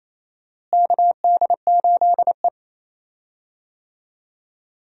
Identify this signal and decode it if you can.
Morse code, heard as a keyed tone, about 27 wpm, 700 Hz. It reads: KD8E
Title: USA Call Signs - 4 Characters - Repeated 2x - 28wpm